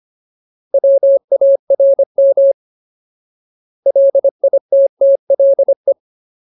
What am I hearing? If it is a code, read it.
Morse code, heard as a keyed tone, about 25 wpm, 550 Hz. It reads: WARM LITTLE
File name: Ultimate Word List / 25WPM Ultimate